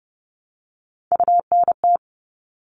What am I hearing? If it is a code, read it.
Morse code, heard as a keyed tone, about 30 wpm, 700 Hz. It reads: UNT
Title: All Cut Numbers - 3-Digit 30wpm